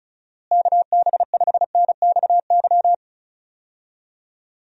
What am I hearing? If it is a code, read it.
Morse code, heard as a keyed tone, about 34 wpm, 700 Hz. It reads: KB5NXY
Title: USA Call Signs - 6 Characters - Repeated 3x - 35wpm